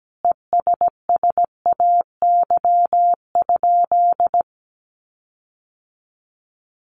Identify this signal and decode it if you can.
Morse code, heard as a keyed tone, about 17 wpm, 700 Hz. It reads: ESSAY?
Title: Sentences from Top 1500 Words 17wpm